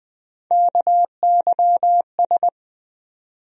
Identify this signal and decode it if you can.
Morse code, heard as a keyed tone, about 20 wpm, 700 Hz. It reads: KYS
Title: USA Call Sign Postfixes - 3 Letters 20wpm